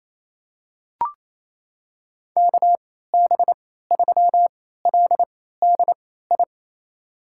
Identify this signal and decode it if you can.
Morse code, heard as a keyed tone, about 28 wpm, spaced as for 16 wpm, 700 Hz. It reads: KB3LDI
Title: USA Call Signs - 6 Characters - 3x Character Spacing - 28wpm